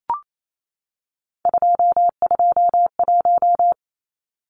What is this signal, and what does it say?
Morse code, heard as a keyed tone, about 28 wpm, 700 Hz. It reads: 221
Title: Numbers - 3-Digit 28wpm